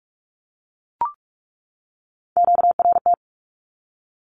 Morse code, heard as a keyed tone, about 44 wpm, 700 Hz. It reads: QRT
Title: Q-Codes 45wpm